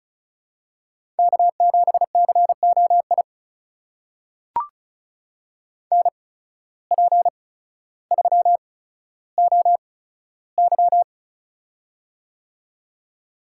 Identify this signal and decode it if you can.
Morse code, heard as a keyed tone, about 36 wpm, spaced as for 15 wpm, 700 Hz. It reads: K7COI NP3OY
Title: USA Call Signs - 8x Character Spacing - 35wpm